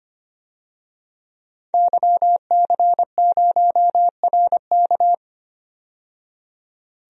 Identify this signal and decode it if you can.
Morse code, heard as a keyed tone, about 25 wpm, 700 Hz. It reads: YC0RK
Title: DXCC-100 Call Signs - 5 Characters - 25wpm